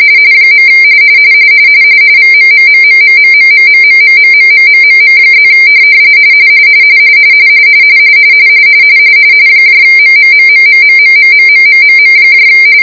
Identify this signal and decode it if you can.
RTTY, about 45.45 baud. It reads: . VMMMMMMMMMMMMMMMMM+ MMMMMMMMMMMMM